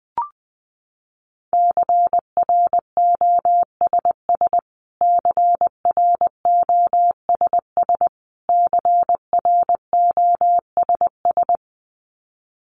Morse code, heard as a keyed tone, about 20 wpm, 700 Hz. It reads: CROSS CROSS CROSS
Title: Top 300 Words Repeated 20wpm